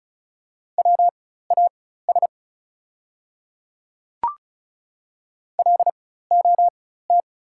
Morse code, heard as a keyed tone, about 34 wpm, spaced as for 16 wpm, 690 Hz. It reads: WAS LOT
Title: 4x Character Spacing - Top 500 Words - 35wpm